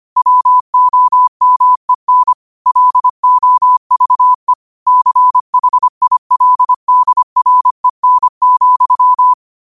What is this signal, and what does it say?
Morse code, heard as a keyed tone, about 25 wpm, 1000 Hz. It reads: WOMEN LOVE CHILDREN,
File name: qotdfiles / 25wpm